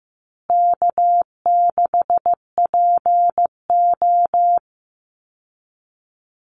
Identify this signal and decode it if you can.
Morse code, heard as a keyed tone, about 15 wpm, 700 Hz. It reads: K6PO
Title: Easy Call Signs Repeated 2x - 15wpm